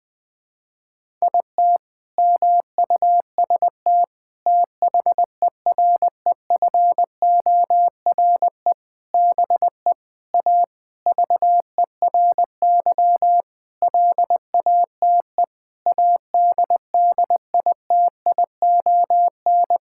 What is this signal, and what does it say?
Morse code, heard as a keyed tone, about 21 wpm, 700 Hz. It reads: IT MUST THEREFORE BE A VERY LATE ADDITION
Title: Sentences from Top 1000 Words - Review Part 6 - 20wpm